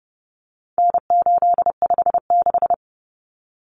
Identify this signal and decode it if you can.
Morse code, heard as a keyed tone, about 30 wpm, 700 Hz. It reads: N856
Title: Common Cut Numbers - 4-Digit 30wpm